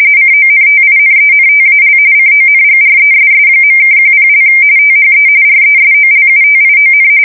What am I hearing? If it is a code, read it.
RTTY, about 45.45 baud. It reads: CONTINENTAL CONGRESS.(10) THE RESOLUTION
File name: fsk170 / usnavy1